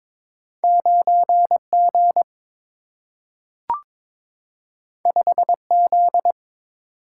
Morse code, heard as a keyed tone, about 22 wpm, 700 Hz. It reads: 9 G 5 Z
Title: Single-Letter Single-Number 22wpm